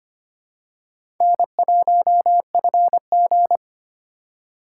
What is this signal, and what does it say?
Morse code, heard as a keyed tone, about 25 wpm, 700 Hz. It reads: N1FG
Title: USA Call Signs - 4 Characters - 3x Character Spacing - 25wpm